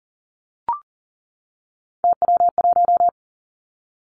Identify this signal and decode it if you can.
Morse code, heard as a keyed tone, about 40 wpm, 700 Hz. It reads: TW1
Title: Three Challenging Letter-Number Combinations 40wpm